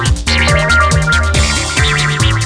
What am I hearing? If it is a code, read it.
There is no code.